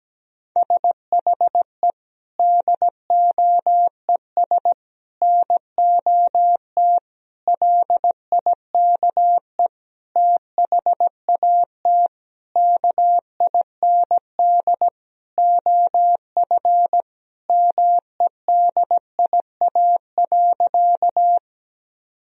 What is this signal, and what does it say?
Morse code, heard as a keyed tone, about 17 wpm, 700 Hz. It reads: SHE DOES NOT LIKE THAT KIND OF MEDIA.
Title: Sentences from Top 1000 Words - Review Part 4 - 17wpm